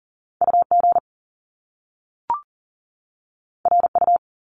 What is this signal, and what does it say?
Morse code, heard as a keyed tone, about 40 wpm, 700 Hz. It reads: UG RU
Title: Top Two Letter Combinations in Words 40wpm